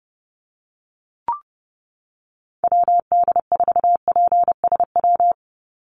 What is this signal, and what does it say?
Morse code, heard as a keyed tone, about 30 wpm, 700 Hz. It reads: WD4PSW
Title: USA Call Signs 30wpm